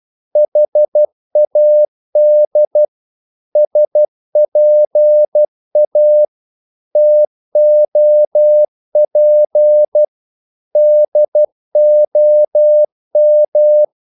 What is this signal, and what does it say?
Morse code, heard as a keyed tone, about 12 wpm, 600 Hz. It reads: HAD SPA TOP DOM
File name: Korte ord DK 12wpm